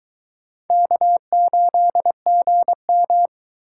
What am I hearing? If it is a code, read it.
Morse code, heard as a keyed tone, about 23 wpm, 700 Hz. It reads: K8GM